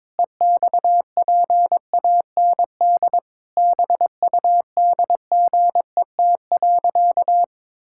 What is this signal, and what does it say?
Morse code, heard as a keyed tone, about 22 wpm, 700 Hz. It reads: EXPAND BUDGET.
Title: Pangram Challenge v2 - 22wpm